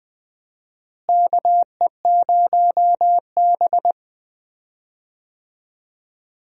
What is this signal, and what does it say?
Morse code, heard as a keyed tone, about 20 wpm, 700 Hz. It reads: KE0B
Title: USA Call Signs - 4 Characters - Repeated 3x - 20wpm